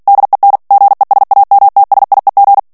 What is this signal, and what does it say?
Morse code, heard as a keyed tone, about 48 wpm, 800 Hz. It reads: DEN GESAMTSIEG
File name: morse mp3 / ALLEMAND texte clair 2 vitesse 45 mots minute